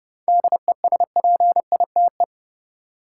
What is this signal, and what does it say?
Morse code, heard as a keyed tone, about 30 wpm, 700 Hz. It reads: DESPITE
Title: Seven Letter Words - 30wpm